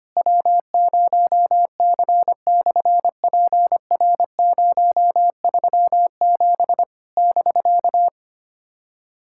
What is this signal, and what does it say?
Morse code, heard as a keyed tone, about 25 wpm, 700 Hz. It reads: W0C/PR037 <BK>
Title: SOTA QSO Practice 25wpm